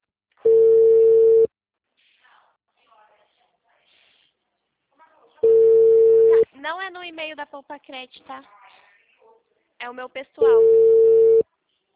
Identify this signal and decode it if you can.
Ringback tone (China, 450 Hz), with speech.